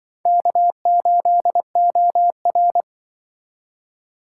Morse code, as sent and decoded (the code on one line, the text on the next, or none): -.- ---.. --- .-.
K8OR